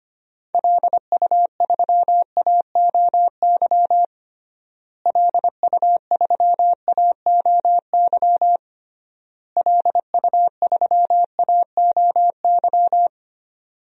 Morse code, heard as a keyed tone, about 25 wpm, 700 Hz. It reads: LU3AOY LU3AOY LU3AOY
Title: DXCC-100 Call Signs - 6 Characters - Repeated 3x - 25wpm